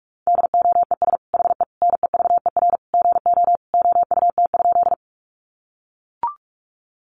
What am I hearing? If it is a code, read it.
Morse code, heard as a keyed tone, about 44 wpm, 700 Hz. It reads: DOES HE NEVER GO OUT?
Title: Sentences from Top 300 Words 45wpm